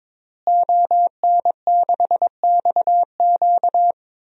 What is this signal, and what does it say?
Morse code, heard as a keyed tone, about 22 wpm, 700 Hz. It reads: ON6XQ
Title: DXCC-100 Call Signs - 5 Characters - Repeated 2x - 22wpm